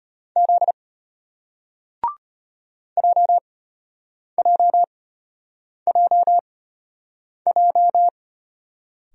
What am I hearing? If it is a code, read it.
Morse code, heard as a keyed tone, about 34 wpm, spaced as for 8 wpm, 700 Hz. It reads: Z JJJJ